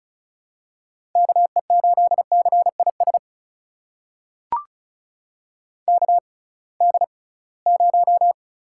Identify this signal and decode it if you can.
Morse code, heard as a keyed tone, about 36 wpm, spaced as for 23 wpm, 690 Hz. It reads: KE8CIS KD0
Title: USA Call Signs - 6 Characters - 6x Character Spacing - 35wpm